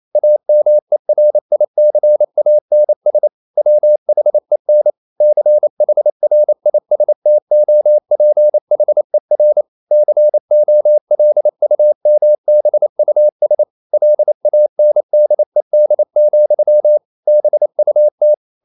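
Morse code, heard as a keyed tone, about 28 wpm, 600 Hz. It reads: AMERICANS WHEN CHRISTOPHER COLUMBUS LANDED, BUT
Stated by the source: Morse code practice at 28 wpm, text of Ken Follett's novel Never in English, from CW med Gnister.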